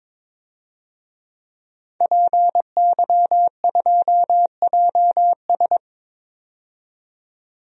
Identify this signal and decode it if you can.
Morse code, heard as a keyed tone, about 22 wpm, 700 Hz. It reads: PY2JS
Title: DXCC-100 Call Signs - 5 Characters - 22wpm